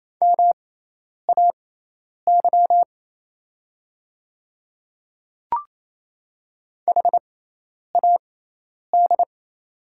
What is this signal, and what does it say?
Morse code, heard as a keyed tone, about 28 wpm, spaced as for 10 wpm, 700 Hz. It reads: MAY HAD
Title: 6x Character Spacing - Top 300 Words - 28wpm